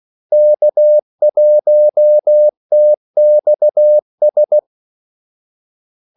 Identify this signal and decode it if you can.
Morse code, heard as a keyed tone, about 16 wpm, 600 Hz. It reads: K1TXS
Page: Kaldesignaler 16 wpm